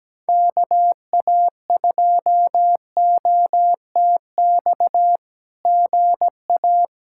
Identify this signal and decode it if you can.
Morse code, heard as a keyed tone, about 17 wpm, 700 Hz. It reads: KA2OTX GA
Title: SOTA QSO Practice 17wpm